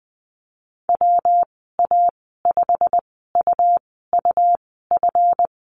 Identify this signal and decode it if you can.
Morse code, heard as a keyed tone, about 20 wpm, spaced as for 14 wpm, 700 Hz. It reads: WA5UUF
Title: 173 Lesson 28g - Number 9 - Call Signs 20wpm